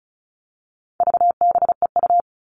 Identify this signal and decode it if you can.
Morse code, heard as a keyed tone, about 34 wpm, 700 Hz. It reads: VBEU
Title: All Cut Numbers - 4-Digit 35wpm